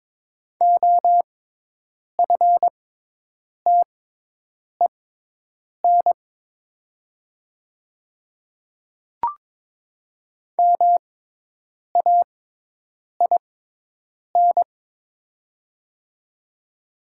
Morse code, heard as a keyed tone, about 22 wpm, spaced as for 8 wpm, 700 Hz. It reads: OFTEN MAIN
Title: 6x Character Spacing - Top 300 Words - 22wpm